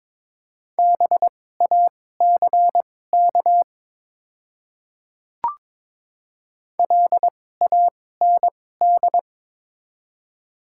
Morse code, heard as a keyed tone, about 22 wpm, spaced as for 16 wpm, 700 Hz. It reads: BACK LAND
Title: 2x Character Spacing - Top 200 Words - 22wpm